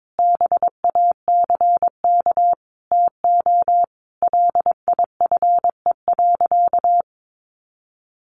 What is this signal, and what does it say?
Morse code, heard as a keyed tone, about 22 wpm, 700 Hz. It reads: BACK TO LIFE.
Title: Sentences from Top 600 Words 22wpm